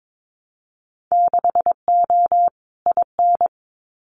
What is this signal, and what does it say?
Morse code, heard as a keyed tone, about 22 wpm, 700 Hz. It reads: 6O IN
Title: Winter Field Day Exchanges Encore 22wpm